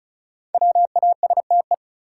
Morse code, heard as a keyed tone, about 36 wpm, 700 Hz. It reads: WASTE
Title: Five Letter Words - 35wpm